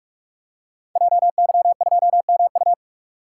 Morse code, heard as a keyed tone, about 44 wpm, 690 Hz. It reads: JY2MU
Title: Code Groups - 5 Letter:Numbers - 45wpm